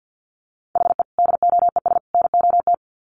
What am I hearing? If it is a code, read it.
Morse code, heard as a keyed tone, about 50 wpm, 700 Hz. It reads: HE DOES NOT
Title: Sets of 3 Words 50wpm